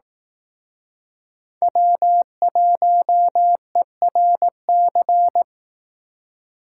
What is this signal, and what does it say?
Morse code, heard as a keyed tone, about 18 wpm, 700 Hz. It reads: W1ERC
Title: Speed Racing - USA 1x3 Call Signs 15wpm